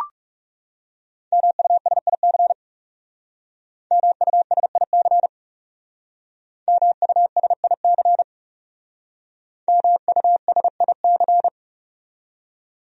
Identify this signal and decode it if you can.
Morse code, heard as a keyed tone, about 36 wpm, 700 Hz. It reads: MUSIC MUSIC MUSIC MUSIC
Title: Speed-Racing Top 500 Words 30wpm